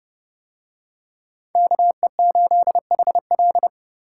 Morse code, heard as a keyed tone, about 29 wpm, 700 Hz. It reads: KE8HL